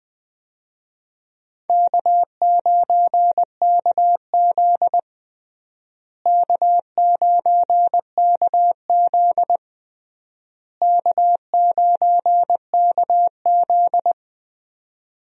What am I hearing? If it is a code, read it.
Morse code, heard as a keyed tone, about 21 wpm, 700 Hz. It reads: K9KZ K9KZ K9KZ
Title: USA Call Signs - 4 Characters - Repeated 3x - 20wpm